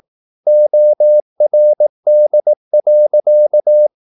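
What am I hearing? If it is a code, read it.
Morse code, heard as a keyed tone, about 18 wpm, 600 Hz. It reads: ORD.
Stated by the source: Slow (QRS) Morse practice at 18 wpm, CW med Gnister.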